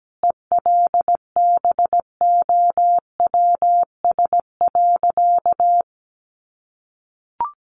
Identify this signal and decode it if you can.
Morse code, heard as a keyed tone, about 17 wpm, 700 Hz. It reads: ELBOWS.
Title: A Sound Of Thunder 17wpm